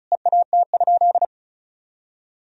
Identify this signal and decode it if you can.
Morse code, heard as a keyed tone, about 36 wpm, 700 Hz. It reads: EAT?
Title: Sentences from Top 300 Words and Abbreviations 35wpm